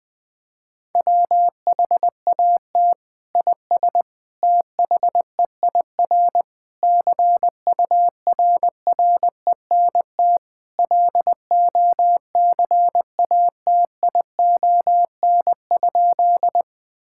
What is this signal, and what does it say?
Morse code, heard as a keyed tone, about 21 wpm, 700 Hz. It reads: WHAT IS THEIR CURRENT LOCATION?
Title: Sentences from Top 1000 Words - Review Part 6 - 20wpm